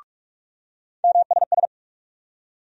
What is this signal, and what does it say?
Morse code, heard as a keyed tone, about 45 wpm, 700 Hz. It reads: MSS
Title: USA Call Sign Postfixes - 3 Letters 45wpm